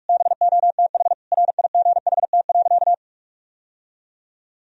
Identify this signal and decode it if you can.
Morse code, heard as a keyed tone, about 41 wpm, 700 Hz. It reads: BOTH RIGHT.
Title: Sentences from Top 1000 Words - Review Part 1 - 45wpm